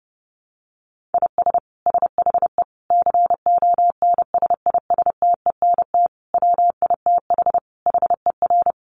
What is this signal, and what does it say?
Morse code, heard as a keyed tone, about 28 wpm, 700 Hz. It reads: IS SHE CONSISTENT WITH HER
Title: Sentences from Top 1500 Words 30wpm